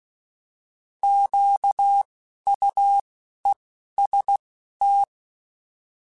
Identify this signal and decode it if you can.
Morse code, heard as a keyed tone, about 16 wpm, 800 Hz. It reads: Q U E S T
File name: Alphabetical / Alphabetical mp3s / Qs @10